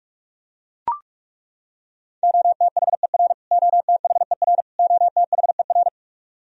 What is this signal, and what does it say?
Morse code, heard as a keyed tone, about 41 wpm, 700 Hz. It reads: OTHER OTHER OTHER
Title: Top 100 Words Repeated 45wpm